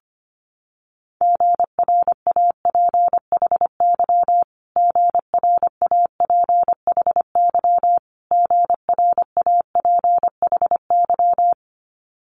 Morse code, heard as a keyed tone, about 25 wpm, 700 Hz. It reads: GRAPHY GRAPHY GRAPHY
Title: Multi-letter Phonemes Repeated 3x 25wpm